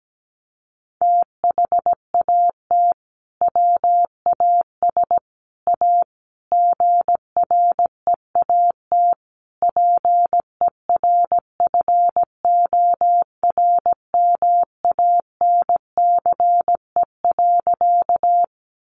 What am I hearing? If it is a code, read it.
Morse code, heard as a keyed tone, about 17 wpm, 700 Hz. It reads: THAT WAS A GREAT PERFORMANCE.
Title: Sentences from Top 1000 Words 17wpm